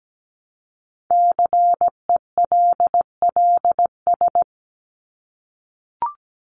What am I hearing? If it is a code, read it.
Morse code, heard as a keyed tone, about 17 wpm, 700 Hz. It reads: CELLS
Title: ES:S Suffix - Top 100 Words with Base Words 17wpm